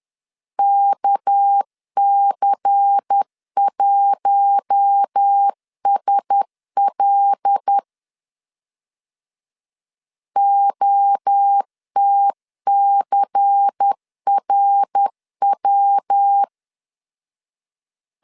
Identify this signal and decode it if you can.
Morse code, heard as a keyed tone, about 10 wpm, 800 Hz. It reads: KC1SL OTCRW
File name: CW / DET / 1 6 1500C S B 10wpm 4137